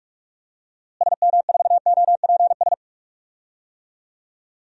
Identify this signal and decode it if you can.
Morse code, heard as a keyed tone, about 46 wpm, 700 Hz. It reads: SM4OPS
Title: DXCC-100 Call Signs - 6 Characters - Repeated 3x - 45wpm